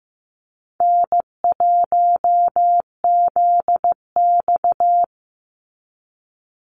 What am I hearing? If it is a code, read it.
Morse code, heard as a keyed tone, about 15 wpm, 700 Hz. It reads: N1ZX
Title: Easy Call Signs Repeated 2x - 15wpm